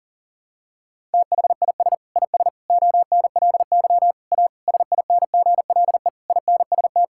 Morse code, heard as a keyed tone, about 40 wpm, 700 Hz. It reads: THIS IS ONLY A SINGLE INST
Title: Sentences from Top 1000 Words - Review Part 5 - 40wpm